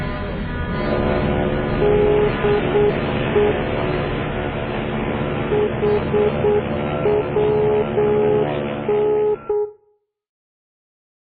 Morse code, heard as a keyed tone, about 8 wpm, 420 Hz, with noise. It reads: DE HWN